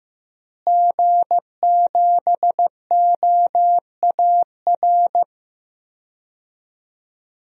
Morse code, heard as a keyed tone, about 15 wpm, 700 Hz. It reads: G7OAR
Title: DXCC-100 Call Signs - 5 Characters - Repeated 3x - 15wpm